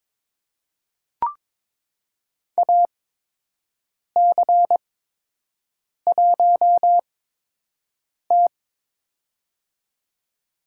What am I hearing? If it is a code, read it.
Morse code, heard as a keyed tone, about 22 wpm, spaced as for 6 wpm, 700 Hz. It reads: AC1T
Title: USA Call Signs - 8x Character Spacing - 22wpm